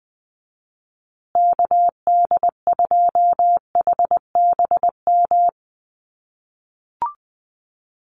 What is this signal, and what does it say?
Morse code, heard as a keyed tone, about 20 wpm, 700 Hz. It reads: KD2HBM
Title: USA Call Signs Repeated 3x 20wpm